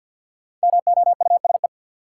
Morse code, heard as a keyed tone, about 52 wpm, 700 Hz. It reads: MOUSE
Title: Sentences from Top 700 Words - New Word Drill - 50wpm